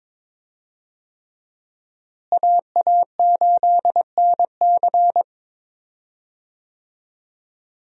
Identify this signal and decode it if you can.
Morse code, heard as a keyed tone, about 22 wpm, 700 Hz. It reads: AA8NC
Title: USA Call Signs - 5 Characters - Repeated 3x - 22wpm